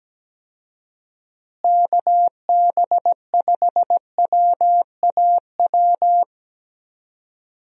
Morse code, heard as a keyed tone, about 17 wpm, 700 Hz. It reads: KB5WAW
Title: USA Call Signs - 6 Characters - Repeated 3x - 17wpm